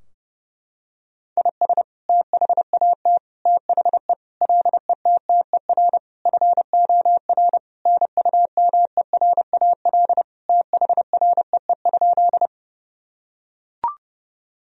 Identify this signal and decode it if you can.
Morse code, heard as a keyed tone, about 30 wpm, 700 Hz. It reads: IS THAT THE LETTER FOR NUMERAL THREE?
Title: Sentences from Top 1000 Words - Review Part 2 - 30wpm